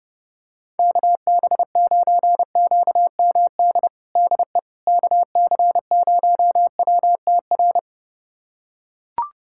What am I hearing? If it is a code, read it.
Morse code, heard as a keyed tone, about 30 wpm, 700 Hz. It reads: KB9QMD DE KC0WTR
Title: USA Callsign DE Callsign 30wpm